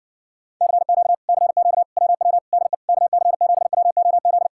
Morse code, heard as a keyed tone, about 60 wpm, 700 Hz. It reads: CQ CQ WW DE KC7WOZ